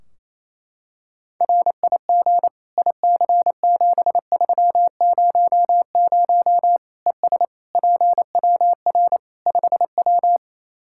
Morse code, heard as a keyed tone, about 28 wpm, 700 Hz. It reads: RIG IC7300 ES PWR 5W
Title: Practice QSO - Exchange 2 - Info - 28wpm